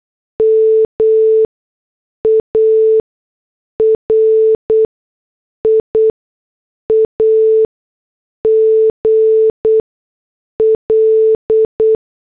Morse code, heard as a keyed tone, about 8 wpm, spaced as for 6 wpm, 440 Hz. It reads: MARIA GL